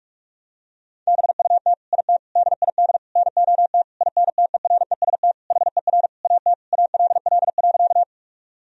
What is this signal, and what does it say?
Morse code, heard as a keyed tone, about 42 wpm, 700 Hz. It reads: BUT IT DID NOT INTEREST HER AT ALL.
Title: Sentences from Top 1000 Words - Review Part 2 - 45wpm